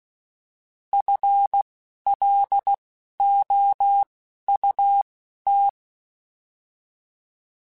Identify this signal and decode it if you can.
Morse code, heard as a keyed tone, about 16 wpm, spaced as for 11 wpm, 800 Hz. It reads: FLOUT